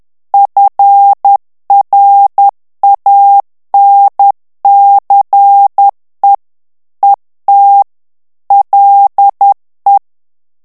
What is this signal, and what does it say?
Morse code, heard as a keyed tone, about 10 wpm, 800 Hz. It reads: FRANCE ET LE
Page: Français - texte en clair 1 - vitesse : 10 mots minute